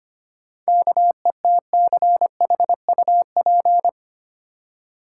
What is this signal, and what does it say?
Morse code, heard as a keyed tone, about 25 wpm, 700 Hz. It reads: KETCHUP
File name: practice / 25wpm / Easy Words with Hints - 25wpm